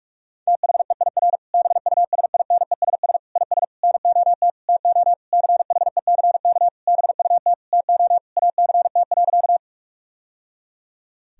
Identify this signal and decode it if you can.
Morse code, heard as a keyed tone, about 41 wpm, 700 Hz. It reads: THEIR BUSINESS IS NOT TO CHECK BUT TO ACT.
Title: Sentences from Top 1000 Words - Review Part 3 - 45wpm